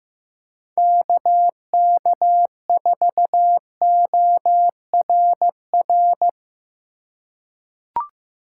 Morse code, heard as a keyed tone, about 15 wpm, 700 Hz. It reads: KK4ORR
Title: USA Call Signs - 6 Characters - 2x Character Spacing - 15wpm